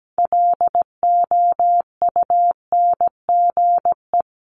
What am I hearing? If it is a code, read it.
Morse code, heard as a keyed tone, about 17 wpm, 700 Hz. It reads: LOUNGE
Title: Multi-letter Phonemes 17wpm